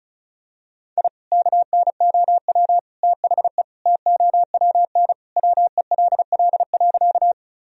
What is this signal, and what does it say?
Morse code, heard as a keyed tone, about 36 wpm, 700 Hz. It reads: I KNOW THE TOWN WELL.
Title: Sentences from Top 1000 Words - Review Part 2 - 35wpm